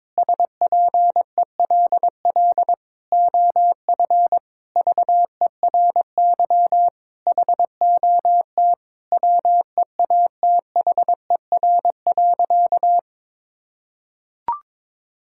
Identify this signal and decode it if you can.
Morse code, heard as a keyed tone, about 22 wpm, 700 Hz. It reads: SPELL OF VERY HOT WEATHER.
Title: Sentences from Top 1000 Words - Review Part 5 - 22wpm